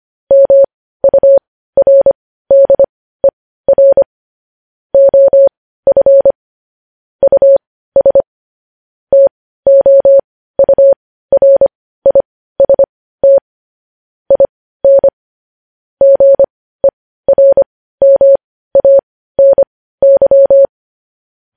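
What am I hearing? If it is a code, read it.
Morse code, heard as a keyed tone, about 25 wpm, spaced as for 15 wpm, 550 Hz. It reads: MURDER OF US TOURIST IN GERMANY
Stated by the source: Morse code news headlines sent at 15.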